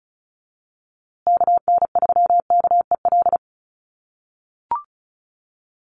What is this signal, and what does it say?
Morse code, heard as a keyed tone, about 34 wpm, 700 Hz. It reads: KN3KEL
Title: USA Call Signs Repeated 3x 35wpm